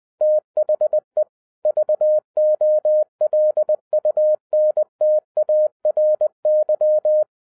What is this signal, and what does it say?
Morse code, heard as a keyed tone, about 20 wpm, 610 Hz. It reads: THE VOLUNTARY